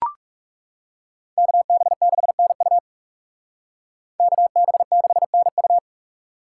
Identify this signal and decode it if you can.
Morse code, heard as a keyed tone, about 40 wpm, 700 Hz. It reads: KB6NU KB6NU